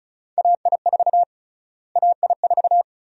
Morse code, heard as a keyed tone, about 36 wpm, 700 Hz. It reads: AI4 AI4
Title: USA Call Sign Prefixes - Repeated 2x - 35wpm